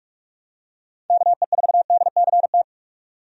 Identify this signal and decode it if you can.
Morse code, heard as a keyed tone, about 48 wpm, 700 Hz. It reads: KE4DCT
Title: USA Call Signs - 6 Characters - 45wpm